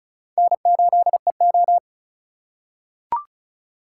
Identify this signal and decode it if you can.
Morse code, heard as a keyed tone, about 36 wpm, 700 Hz. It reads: N8EO
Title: USA Call Signs - 4 Characters - 4x Character Spacing - 35wpm